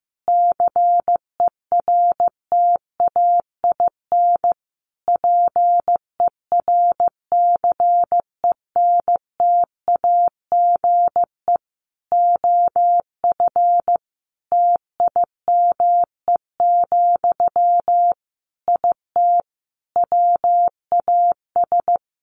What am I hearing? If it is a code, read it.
Morse code, heard as a keyed tone, about 15 wpm, 700 Hz. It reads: CERTAIN PERCENTAGE OF TIME, IT WAS
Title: Sentences from Top 1300 Words 15wpm